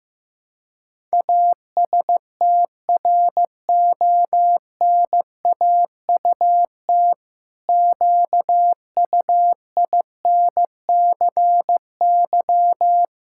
Morse code, heard as a keyed tone, about 15 wpm, 700 Hz. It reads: ASTRONAUT QUINCY
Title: Pangram Challenge v2 - 15wpm